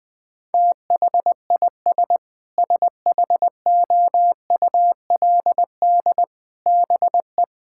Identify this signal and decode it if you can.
Morse code, heard as a keyed tone, about 20 wpm, 700 Hz. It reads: THIS SHOULD BE
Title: Sentences from Top 1000 Words - Review Part 6 - 20wpm